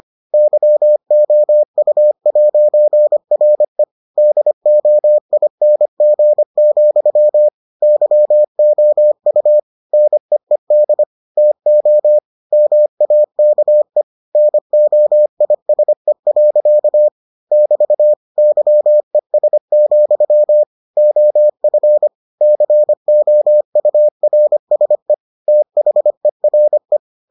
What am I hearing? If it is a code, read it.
Morse code, heard as a keyed tone, about 25 wpm, 600 Hz. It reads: YOU'RE DOING, YOU NEED TO MAKE NOISE. <BT> YES, OF COURSE THERE